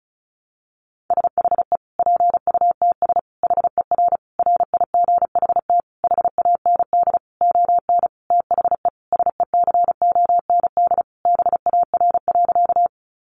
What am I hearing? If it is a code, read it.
Morse code, heard as a keyed tone, about 34 wpm, 700 Hz. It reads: SHE PUTS HER RIGHT HAND ON THE SECOND BAR.
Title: Easy Phrases 35wpm